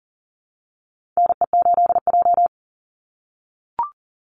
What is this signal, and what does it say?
Morse code, heard as a keyed tone, about 40 wpm, 700 Hz. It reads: NE8J
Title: Call Signs - 40wpm